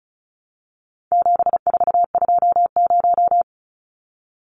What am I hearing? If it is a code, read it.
Morse code, heard as a keyed tone, about 34 wpm, 700 Hz. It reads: 7420
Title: Numbers - 4-Digit 35wpm